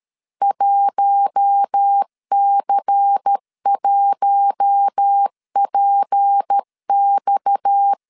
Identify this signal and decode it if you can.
Morse code, heard as a keyed tone, about 13 wpm, 800 Hz. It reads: 1C1PX
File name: CW / DET / 1 6 1500C S A 12wpm 3440